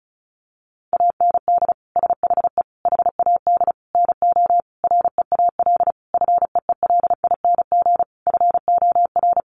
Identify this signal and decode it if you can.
Morse code, heard as a keyed tone, about 34 wpm, 700 Hz. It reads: AND SHE HAD NO REAL FEELING FOR
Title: Sentences from Top 1200 Words 35wpm